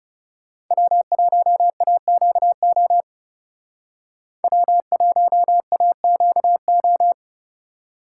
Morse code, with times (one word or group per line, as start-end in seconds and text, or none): 0.7-3.0 W1AQO
4.4-7.1 W1AQO